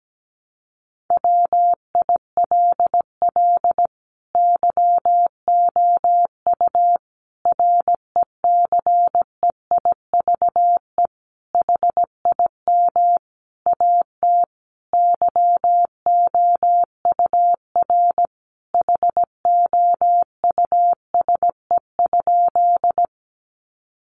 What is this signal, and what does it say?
Morse code, heard as a keyed tone, about 17 wpm, 700 Hz. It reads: WILL YOU RECEIVE HIM AT YOUR HOUSE?
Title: Sentences from Top 1000 Words 17wpm